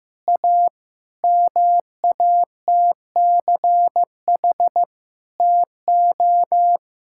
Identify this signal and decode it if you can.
Morse code, heard as a keyed tone, about 15 wpm, 700 Hz. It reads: A MATCH TO
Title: Sentences from Top 1600 Words 15wpm